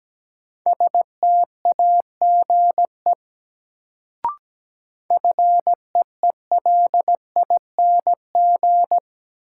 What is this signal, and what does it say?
Morse code, heard as a keyed tone, about 17 wpm, 700 Hz. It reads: STAGE FEELING
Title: Sentences from Top 1200 Words - New Word Drill Alt - 17wpm